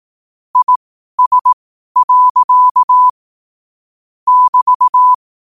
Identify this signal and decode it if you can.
Morse code, heard as a keyed tone, about 18 wpm, spaced as for 12 wpm, 1000 Hz. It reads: IS. <BT>
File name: qotdfiles / 12wpm